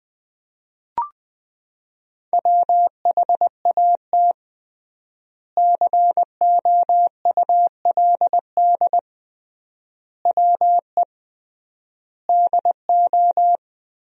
Morse code, heard as a keyed tone, about 19 wpm, 700 Hz. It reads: WHAT COULD WE DO
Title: 3x Word-Spacing - Sentences from Top 400 Words - 20wpm